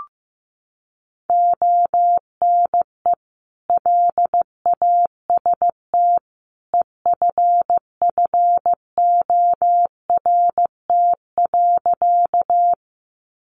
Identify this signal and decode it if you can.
Morse code, heard as a keyed tone, about 15 wpm, 700 Hz. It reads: ONE LAST EFFORT.
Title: Sentences from Top 1200 Words 15wpm